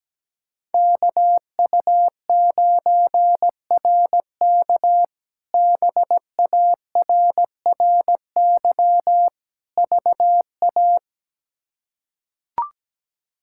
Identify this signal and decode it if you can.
Morse code, heard as a keyed tone, about 17 wpm, 700 Hz. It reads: KU9RK BARRY VA
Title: K1USN SST QSO Practice - 17wpm